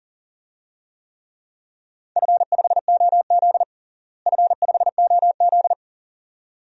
Morse code, heard as a keyed tone, about 40 wpm, 700 Hz. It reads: F5OZ F5OZ
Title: DXCC-100 Call Signs - 4 Characters - Repeated 2x - 40wpm